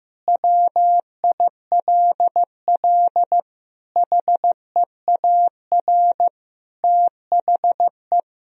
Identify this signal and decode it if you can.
Morse code, heard as a keyed tone, about 15 wpm, 700 Hz. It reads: WILL HEAR THE
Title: Sentences from Top 1000 Words - Review Part 1 - 15wpm